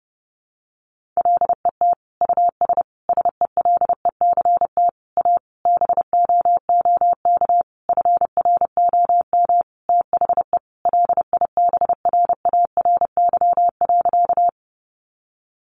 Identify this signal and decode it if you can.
Morse code, heard as a keyed tone, about 30 wpm, 700 Hz. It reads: LET US SELECT A BOOK FROM THE LIBRARY.
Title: Sentences from Top 1300 Words 30wpm